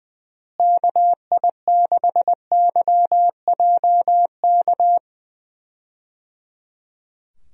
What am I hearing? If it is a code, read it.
Morse code, heard as a keyed tone, about 20 wpm, 700 Hz. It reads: KI6YJK
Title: USA Call Signs - 6 Characters - Repeated 3x - 20wpm